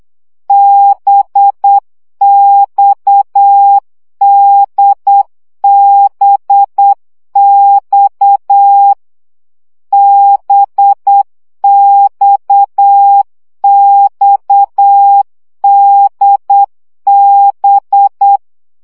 Morse code, heard as a keyed tone, about 8 wpm, 800 Hz. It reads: BXDBX BXXDB
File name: CW gyakorlas d b x 7 baud